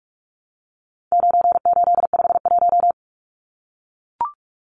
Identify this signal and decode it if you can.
Morse code, heard as a keyed tone, about 44 wpm, 700 Hz. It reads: 9851
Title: Numbers - 4-Digit 45wpm